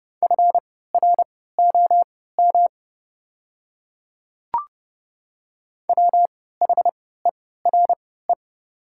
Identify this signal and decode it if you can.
Morse code, heard as a keyed tone, about 30 wpm, spaced as for 17 wpm, 700 Hz. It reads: FROM WHERE
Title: 3x Character Spacing - Top 500 Words - 30wpm